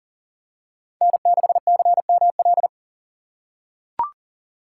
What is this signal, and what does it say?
Morse code, heard as a keyed tone, about 40 wpm, 700 Hz. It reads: N6CML
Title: USA Call Signs - 5 Characters - 2x Character Spacing - 40wpm